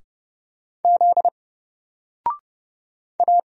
Morse code, heard as a keyed tone, about 29 wpm, 700 Hz. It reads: Z A
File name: Single-Letter-Number-30wpm